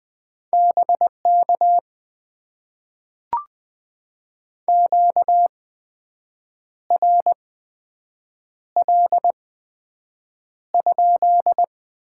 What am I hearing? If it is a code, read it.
Morse code, heard as a keyed tone, about 20 wpm, spaced as for 7 wpm, 700 Hz. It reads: BK QRL?
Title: Top 90 QSO Elements - 8x Character Spacing - 20wpm